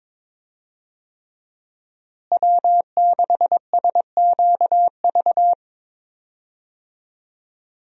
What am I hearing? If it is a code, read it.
Morse code, heard as a keyed tone, about 21 wpm, 700 Hz. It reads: W6SQV
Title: USA Call Signs - 5 Characters - Repeated 3x - 22wpm